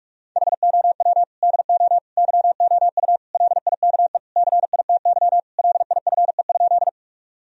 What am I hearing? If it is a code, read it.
Morse code, heard as a keyed tone, about 42 wpm, 700 Hz. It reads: HOW DO YOU LIKE CITY LIFE?
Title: Sentences from Top 1000 Words - Review Part 1 - 45wpm